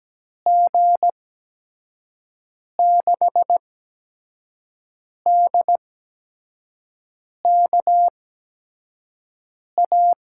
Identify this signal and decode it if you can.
Morse code, heard as a keyed tone, about 17 wpm, spaced as for 5 wpm, 700 Hz. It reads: G6DKA
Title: USA Call Signs - 6 Characters - 8x Character Spacing - 17wpm